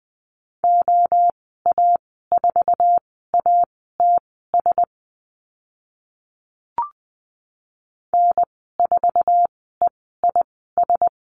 Morse code, heard as a keyed tone, about 20 wpm, spaced as for 15 wpm, 700 Hz. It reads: OA4ATS N4EIS
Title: 069 Lesson 12g - Letter d - Call Signs 20wpm